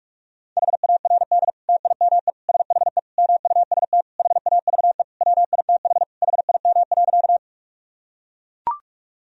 Morse code, heard as a keyed tone, about 42 wpm, 700 Hz. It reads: HARD TIME SHE MUST HAVE WITH HIM.
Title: Sentences from Top 1000 Words - Review Part 1 - 45wpm